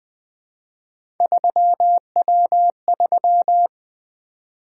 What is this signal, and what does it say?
Morse code, heard as a keyed tone, about 20 wpm, 700 Hz. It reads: 3W3
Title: Three Letter-Number Combinations - 20wpm